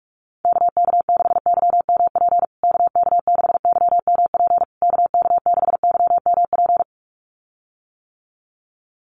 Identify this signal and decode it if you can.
Morse code, heard as a keyed tone, about 44 wpm, 700 Hz. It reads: KK6YMP KK6YMP KK6YMP
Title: USA Call Signs Repeated 3x 45wpm